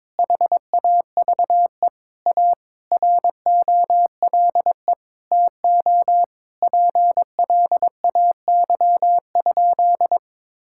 Morse code, heard as a keyed tone, about 22 wpm, 700 Hz. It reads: HAVE A ROLE TO PLAY?
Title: Sentences from Top 1000 Words - Review Part 6 - 22wpm